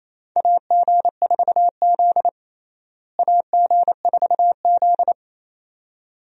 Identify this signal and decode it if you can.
Morse code, heard as a keyed tone, about 28 wpm, 700 Hz. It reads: AG4Z AG4Z
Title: USA Call Signs - 4 Characters - Repeated 2x - 28wpm